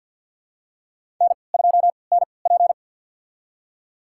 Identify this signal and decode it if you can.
Morse code, heard as a keyed tone, about 51 wpm, spaced as for 28 wpm, 700 Hz. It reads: N2NP
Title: USA Call Signs - 4 Characters - 3x Character Spacing - 50wpm